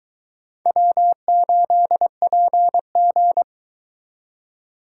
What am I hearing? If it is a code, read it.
Morse code, heard as a keyed tone, about 23 wpm, 700 Hz. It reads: W8PG